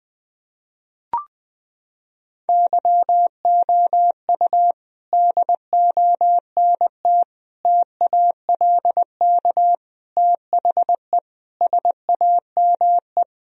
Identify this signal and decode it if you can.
Morse code, heard as a keyed tone, about 21 wpm, 700 Hz. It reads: YOU DONT TALK THE SAME
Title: Sentences from Top 1000 Words - Review Part 3 - 20wpm